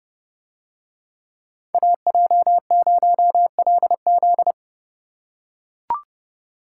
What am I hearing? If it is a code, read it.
Morse code, heard as a keyed tone, about 29 wpm, 700 Hz. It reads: AJ0LZ